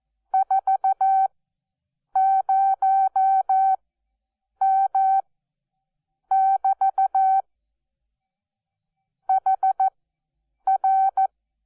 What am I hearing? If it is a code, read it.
Morse code, heard as a keyed tone, about 14 wpm, spaced as for 7 wpm, 770 Hz. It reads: 40M<BT> HR
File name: Onlinelehrgang / morsen / DJ4UF Morsen12